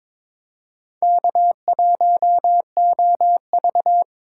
Morse code, heard as a keyed tone, about 22 wpm, 700 Hz. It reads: K1OV
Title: USA Call Signs - 4 Characters - 3x Character Spacing - 22wpm